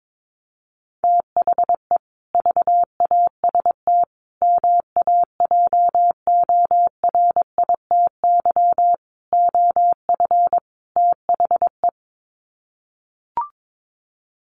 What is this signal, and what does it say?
Morse code, heard as a keyed tone, about 22 wpm, 700 Hz. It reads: THE VAST MAJORITY OF THE
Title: Sets of 5 Words Encore 22wpm